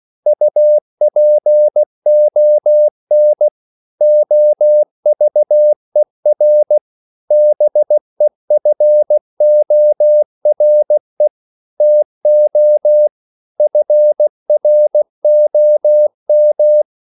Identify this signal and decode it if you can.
Morse code, heard as a keyed tone, about 16 wpm, 600 Hz. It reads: UPON OVER BEFORE TO FROM
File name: CW / Practice / 500MostCommonEnglishWords16